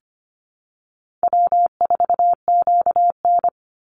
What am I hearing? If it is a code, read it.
Morse code, heard as a keyed tone, about 25 wpm, 700 Hz. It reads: W4QN
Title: Easy Call Signs Repeated 2x - 25wpm